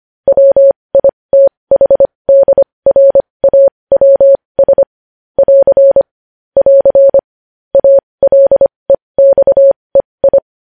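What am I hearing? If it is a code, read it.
Morse code, heard as a keyed tone, about 25 wpm, spaced as for 20 wpm, 550 Hz. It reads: WITHDRAWS <AR> <AR> ALEXEI
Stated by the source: Morse practice of news headlines at 20 wpm.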